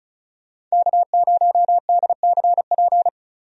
Morse code, heard as a keyed tone, about 36 wpm, 700 Hz. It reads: K0DCP
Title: USA Call Signs - 5 Characters - Repeated 2x - 35wpm